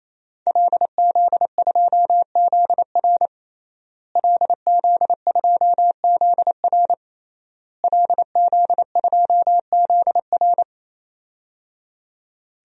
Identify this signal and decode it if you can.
Morse code, heard as a keyed tone, about 28 wpm, 700 Hz. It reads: LZ2ZR LZ2ZR LZ2ZR
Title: DXCC-100 Call Signs - 5 Characters - Repeated 3x - 28wpm